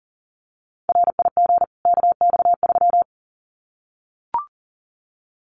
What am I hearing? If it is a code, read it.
Morse code, heard as a keyed tone, about 39 wpm, 700 Hz. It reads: RIG KX3
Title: Common QSO Phrases 40wpm